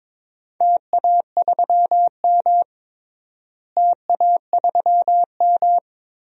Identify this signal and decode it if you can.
Morse code, heard as a keyed tone, about 21 wpm, 700 Hz. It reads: TA3M TA3M
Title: DXCC-100 Call Signs - 4 Characters - Repeated 3x - 22wpm